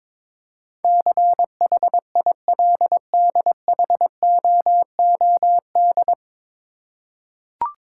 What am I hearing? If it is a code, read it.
Morse code, heard as a keyed tone, about 22 wpm, 700 Hz. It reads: CHILDHOOD
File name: 9 Letter Words - First 3 Letters Given - 22wpm